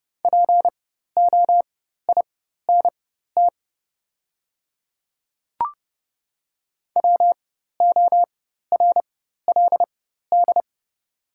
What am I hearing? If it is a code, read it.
Morse code, heard as a keyed tone, about 30 wpm, spaced as for 14 wpm, 700 Hz. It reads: POINT WORLD
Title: 4x Character Spacing - Top 300 Words - 30wpm